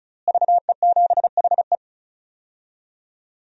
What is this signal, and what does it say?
Morse code, heard as a keyed tone, about 36 wpm, 700 Hz. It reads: VE7HE
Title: DXCC-100 Call Signs - 5 Characters - Repeated 2x - 35wpm